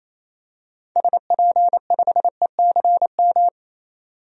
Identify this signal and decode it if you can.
Morse code, heard as a keyed tone, about 28 wpm, 700 Hz. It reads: SP5ECM
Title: DXCC-100 Call Signs - 6 Characters - 28wpm